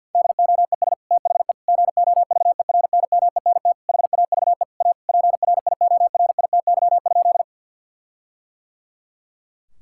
Morse code, heard as a keyed tone, about 50 wpm, 700 Hz. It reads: DOES THE GOVERNMENT HAVE A PRIORITY?
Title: Sentences from Top 1600 Words 50wpm